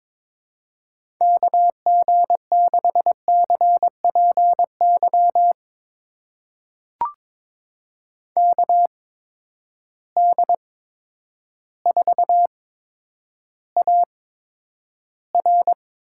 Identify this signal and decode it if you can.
Morse code, heard as a keyed tone, about 22 wpm, spaced as for 10 wpm, 700 Hz. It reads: KG6CPY KD4AR
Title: USA Call Signs - 6 Characters - 8x Character Spacing - 22wpm